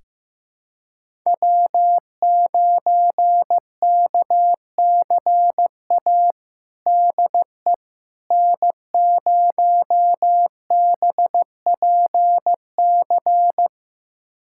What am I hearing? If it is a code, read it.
Morse code, heard as a keyed tone, about 15 wpm, 700 Hz. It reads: W9KCA DE N0BPC
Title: USA Callsign DE Callsign - 5 Characters - 15wpm